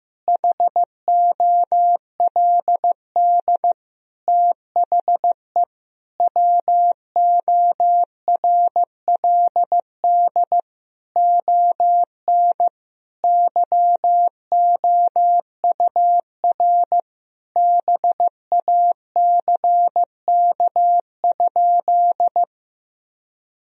Morse code, heard as a keyed tone, about 16 wpm, 700 Hz. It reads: HOLD THE WORLD ON YOUR BACK?
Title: Sentences from Top 1000 Words - Review Part 2 - 15wpm